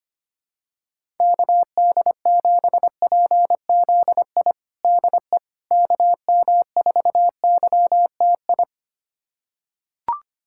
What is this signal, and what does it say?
Morse code, heard as a keyed tone, about 25 wpm, 700 Hz. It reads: KD7PZI DE KM4YTI
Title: USA Callsign DE Callsign - 6 Characters - 25wpm